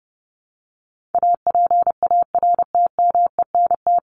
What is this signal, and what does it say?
Morse code, heard as a keyed tone, about 28 wpm, 700 Hz. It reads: APARTMENT
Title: Top 751-1000 Words 30wpm